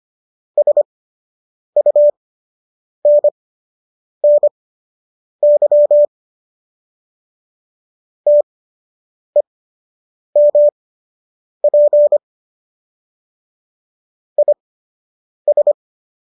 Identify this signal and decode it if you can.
Morse code, heard as a keyed tone, about 25 wpm, spaced as for 8 wpm, 590 Hz. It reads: SUNNY TEMP IS
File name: qso006 08